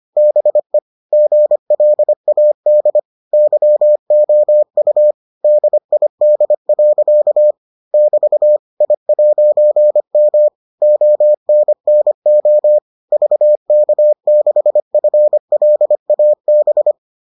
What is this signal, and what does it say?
Morse code, heard as a keyed tone, about 25 wpm, 600 Hz. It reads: BE GLAD YOU DID. <BT> I'M ONNO VK6FLAB